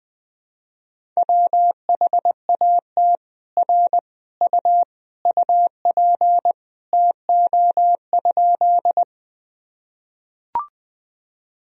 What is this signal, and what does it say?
Morse code, heard as a keyed tone, about 20 wpm, 700 Hz. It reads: WHAT R U UP TO?
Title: Sentences from Top 100 Words and Abbreviations 20wpm